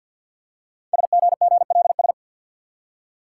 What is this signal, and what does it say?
Morse code, heard as a keyed tone, about 50 wpm, 690 Hz. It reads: SGGLS
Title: Code Groups - 5 Letters - 50wpm